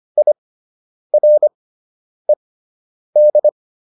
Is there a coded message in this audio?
Yes: Morse code from 0.2 to 3.5 s, heard as a keyed tone, about 25 wpm, spaced as for 9 wpm, 600 Hz. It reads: IRED